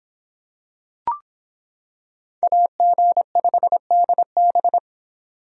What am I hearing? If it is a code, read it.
Morse code, heard as a keyed tone, about 26 wpm, 700 Hz. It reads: AG5DB